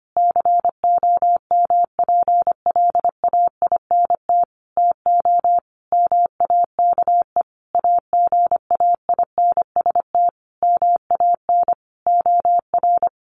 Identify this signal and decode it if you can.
Morse code, heard as a keyed tone, about 25 wpm, 700 Hz. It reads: COMPLAINT TO MAKE AGAINST MAN OR
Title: Sentences from Top 1400 Words 25wpm